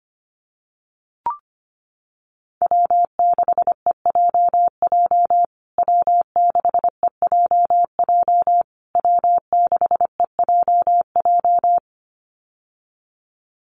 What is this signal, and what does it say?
Morse code, heard as a keyed tone, about 25 wpm, 700 Hz. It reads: W6EJJ W6EJJ W6EJJ
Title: USA Call Signs Repeated 3x 25wpm